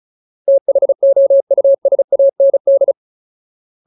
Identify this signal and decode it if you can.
Morse code, heard as a keyed tone, about 34 wpm, 550 Hz. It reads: THOUSAND
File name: Ultimate Word List / 35WPM Ultimate